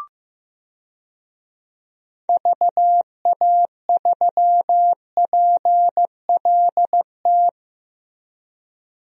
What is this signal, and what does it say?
Morse code, heard as a keyed tone, about 15 wpm, 700 Hz. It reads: VA3PLT
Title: DXCC-100 Call Signs - 6 Characters - Repeated 3x - 15wpm